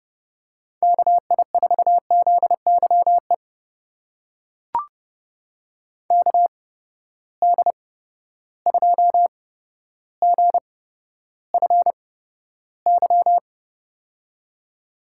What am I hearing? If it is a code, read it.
Morse code, heard as a keyed tone, about 30 wpm, spaced as for 13 wpm, 700 Hz. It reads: KI4ZYE KD2GFY
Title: USA Call Signs - 8x Character Spacing - 30wpm